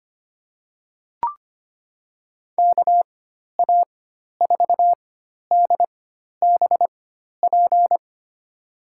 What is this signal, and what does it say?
Morse code, heard as a keyed tone, about 25 wpm, spaced as for 12 wpm, 700 Hz. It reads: KA4DBP
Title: USA Call Signs - 6 Characters - 4x Character Spacing - 25wpm